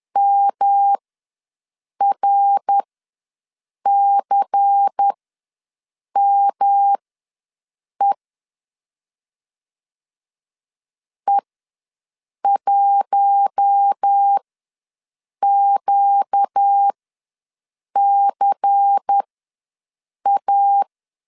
Morse code, heard as a keyed tone, about 10 wpm, spaced as for 6 wpm, 800 Hz. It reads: MRCME E1QCA
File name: LEZIONI / 5 LBVC / 5 LBVC ws 10wpm 3733